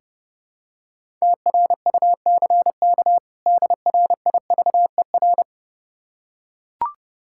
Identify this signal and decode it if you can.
Morse code, heard as a keyed tone, about 32 wpm, 700 Hz. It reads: TRUCK DRIVER
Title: Practice QSO - Exchange 3 - More Info - 30wpm